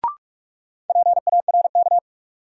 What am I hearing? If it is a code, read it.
Morse code, heard as a keyed tone, about 46 wpm, 700 Hz. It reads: PARK
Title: Sentences from Top 1300 Words - New Word Drill Alt - 45wpm